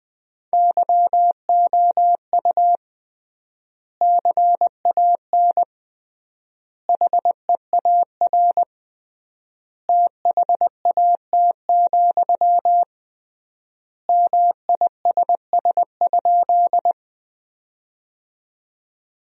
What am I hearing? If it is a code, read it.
Morse code, heard as a keyed tone, about 20 wpm, 700 Hz. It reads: YOU CAN HEAR THAT, MISS?
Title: 3x Word-Spacing - Sentences from Top 500 Words - 20wpm